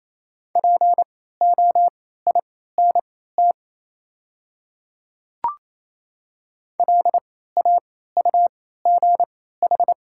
Morse code, heard as a keyed tone, about 28 wpm, spaced as for 16 wpm, 700 Hz. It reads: POINT LAUGH
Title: 3x Character Spacing - Top 500 Words - 28wpm